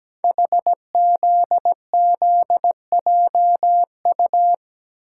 Morse code, heard as a keyed tone, about 17 wpm, 700 Hz. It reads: HZZJU
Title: Code Groups - 5 Letters - 17wpm